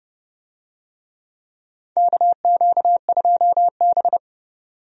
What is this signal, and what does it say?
Morse code, heard as a keyed tone, about 30 wpm, 700 Hz. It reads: KQ2B
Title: USA Call Signs - 4 Characters - 30wpm